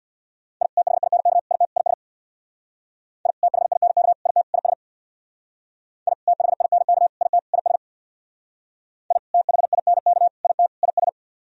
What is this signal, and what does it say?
Morse code, heard as a keyed tone, about 62 wpm, 700 Hz. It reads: I THINK IT IS I THINK IT IS I THINK IT IS I THINK IT IS
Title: Speed-Racing - Sets of 4 Words 50wpm